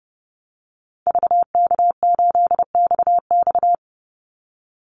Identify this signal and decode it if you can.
Morse code, heard as a keyed tone, about 30 wpm, 700 Hz. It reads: VK8XX
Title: Call Signs Repeated 2x - 30wpm